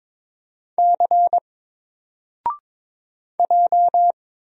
Morse code, heard as a keyed tone, about 22 wpm, 700 Hz. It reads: C J
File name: Single-Letter-Number-22wpm